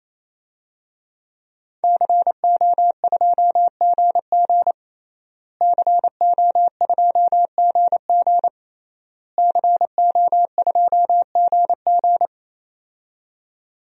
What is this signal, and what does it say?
Morse code, heard as a keyed tone, about 28 wpm, 700 Hz. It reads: CO2GG CO2GG CO2GG
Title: DXCC-100 Call Signs - 5 Characters - Repeated 3x - 28wpm